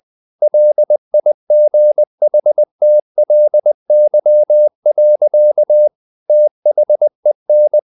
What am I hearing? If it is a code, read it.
Morse code, heard as a keyed tone, about 20 wpm, 600 Hz. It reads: LIGHTLY. THEN
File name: the old man and the sea / the old man and the sea part 1 20wpm